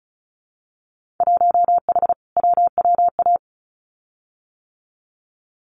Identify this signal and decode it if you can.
Morse code, heard as a keyed tone, about 36 wpm, 700 Hz. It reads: 1H WWA
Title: Winter Field Day Exchanges 35wpm